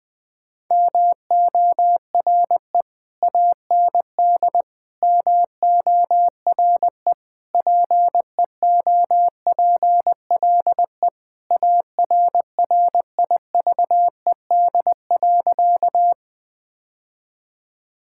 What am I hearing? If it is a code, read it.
Morse code, heard as a keyed tone, about 21 wpm, 700 Hz. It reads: MORE AND MORE PEOPLE ARRIVED.
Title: Sentences with 3-Word Phrases - 20wpm